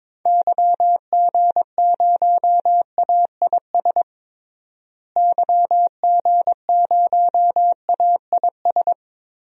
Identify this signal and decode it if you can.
Morse code, heard as a keyed tone, about 22 wpm, 700 Hz. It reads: YG0AIS YG0AIS
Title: DXCC-100 Call Signs - 6 Characters - Repeated 3x - 22wpm